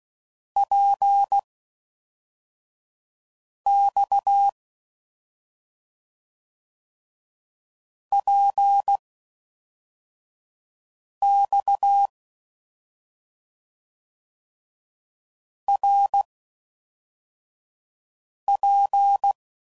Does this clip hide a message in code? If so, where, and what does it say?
Yes: Morse code from 0.6 to 19.3 s, heard as a keyed tone, about 16 wpm, spaced as for 4 wpm, 800 Hz. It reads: PX PX RP